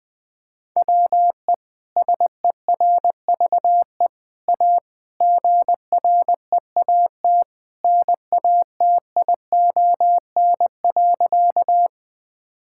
Morse code, heard as a keyed tone, about 20 wpm, 700 Hz. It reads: WE SERVE A GREAT NATION.
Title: Sentences from Top 1000 Words - Review Part 6 - 20wpm